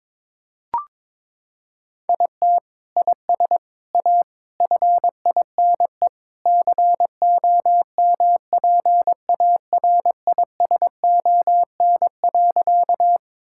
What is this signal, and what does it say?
Morse code, heard as a keyed tone, about 21 wpm, 700 Hz. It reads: IT IS A FINE COMPARISON.
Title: Sentences from Top 1600 Words 22wpm